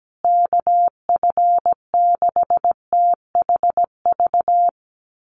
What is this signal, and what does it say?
Morse code, heard as a keyed tone, about 17 wpm, 700 Hz. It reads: KF6THV
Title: USA Call Signs Repeated 2x 17wpm